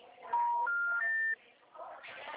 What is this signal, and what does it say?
Special information tone, with noise.